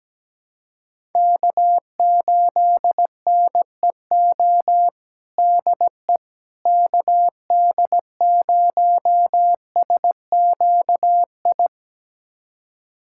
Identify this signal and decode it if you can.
Morse code, heard as a keyed tone, about 17 wpm, 700 Hz. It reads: K8NEO DE KD0SQI
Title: USA Callsign DE Callsign 17wpm